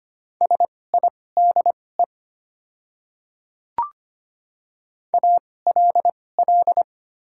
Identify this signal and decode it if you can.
Morse code, heard as a keyed tone, about 25 wpm, spaced as for 18 wpm, 690 Hz. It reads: SIDE ALL
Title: 2x Character Spacing - Top 100 Words - 25wpm